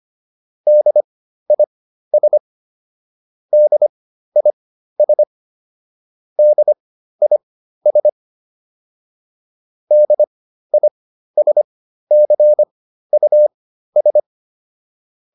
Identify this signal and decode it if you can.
Morse code, heard as a keyed tone, about 25 wpm, spaced as for 13 wpm, 600 Hz. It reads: DIS DIS DIS DISCUS
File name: DIS6 13